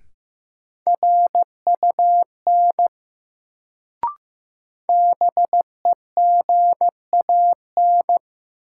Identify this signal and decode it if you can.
Morse code, heard as a keyed tone, about 15 wpm, 700 Hz. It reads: RUN BEGAN
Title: Sentences from Top 300 Words - New Word Drill - 15wpm